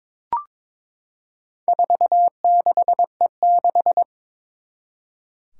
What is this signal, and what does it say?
Morse code, heard as a keyed tone, about 22 wpm, 700 Hz. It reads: 46E6
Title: Four Challenging Letter-Number Combinations 22wpm